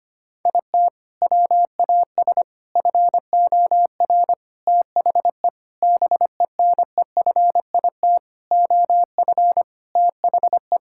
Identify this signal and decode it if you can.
Morse code, heard as a keyed tone, about 25 wpm, 700 Hz. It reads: IT WAS FOR THE BENEFIT OF THE
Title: Sentences from Top 1000 Words - Review Part 6 - 25wpm